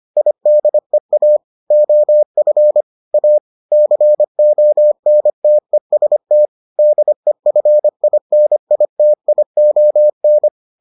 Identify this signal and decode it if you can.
Morse code, heard as a keyed tone, about 25 wpm, 600 Hz. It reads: IDEA OF A CONTEST DEFINITION